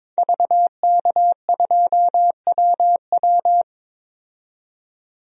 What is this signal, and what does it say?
Morse code, heard as a keyed tone, about 22 wpm, 700 Hz. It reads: VK2WW
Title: DXCC-100 Call Signs - 5 Characters - 22wpm